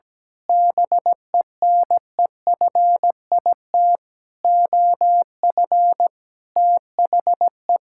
Morse code, heard as a keyed tone, about 17 wpm, 700 Hz. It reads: BENEFIT OF THE
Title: Sentences from Top 1000 Words - Review Part 6 - 17wpm